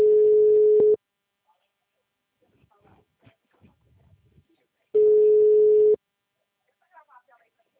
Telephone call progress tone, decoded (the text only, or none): ringback tone